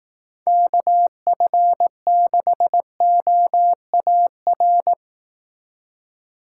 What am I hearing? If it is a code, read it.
Morse code, heard as a keyed tone, about 18 wpm, 700 Hz. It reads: KF6OAR
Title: Speed Racing - USA 2x3 Call Signs 15wpm